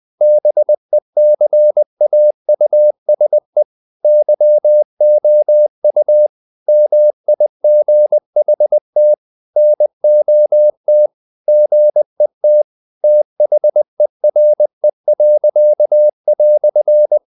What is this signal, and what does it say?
Morse code, heard as a keyed tone, about 20 wpm, 600 Hz. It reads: BECAUSE YOU MIGHT NOT GET THERE."
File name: CW / Practice / EnglishQuotations0320